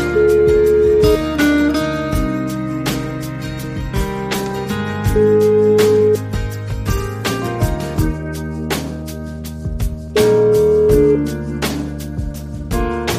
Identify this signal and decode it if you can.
Ringback tone (Europe, 425 Hz), with noise.